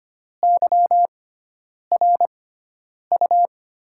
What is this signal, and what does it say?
Morse code, heard as a keyed tone, about 25 wpm, spaced as for 9 wpm, 700 Hz. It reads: YRU
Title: USA Call Signs - 6 Characters - 6x Character Spacing - 25wpm